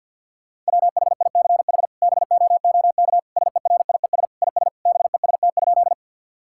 Morse code, heard as a keyed tone, about 50 wpm, 700 Hz. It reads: WHICH BOOK SERIES IS BEST?
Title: Sentences from Top 1000 Words - Review Part 5 - 50wpm